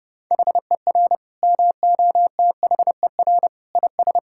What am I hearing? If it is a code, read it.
Morse code, heard as a keyed tone, about 30 wpm, 700 Hz. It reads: HER MOTHER IS
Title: Sentences from Top 1000 Words - Review Part 5 - 30wpm